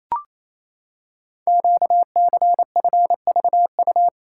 Morse code, heard as a keyed tone, about 27 wpm, 700 Hz. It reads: QCFVU
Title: Code Groups - 5 Letters - 28wpm